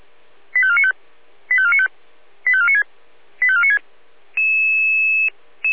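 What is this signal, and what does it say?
Five-tone selective call (ZVEI), heard as quick runs of five tones, tones of about 70 ms. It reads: 85495 85495 85496 85496
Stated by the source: ZVEI five-tone selective call 85493 received by SDR.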